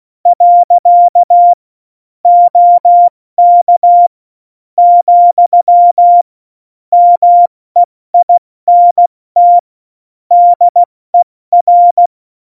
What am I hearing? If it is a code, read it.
Morse code, heard as a keyed tone, about 16 wpm, 700 Hz. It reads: . OK , MEINT DER
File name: Morsekurs / Witze / Witz 23. Oktober 2018, 70 F80